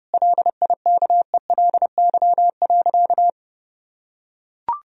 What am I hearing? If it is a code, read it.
Morse code, heard as a keyed tone, about 32 wpm, 700 Hz. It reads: LIKELY.
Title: Sentences from Top 1000 Words - Review Part 4 - 30wpm